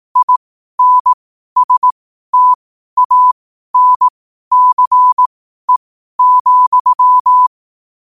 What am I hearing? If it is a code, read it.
Morse code, heard as a keyed tone, about 18 wpm, spaced as for 12 wpm, 1000 Hz. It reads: INSTANCE,
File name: qotdfiles / 12wpm